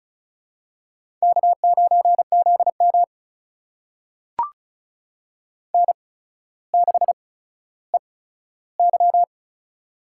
Morse code, heard as a keyed tone, about 36 wpm, spaced as for 15 wpm, 700 Hz. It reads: K9ZM N6EY
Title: USA Call Signs - 4 Characters - 8x Character Spacing - 35wpm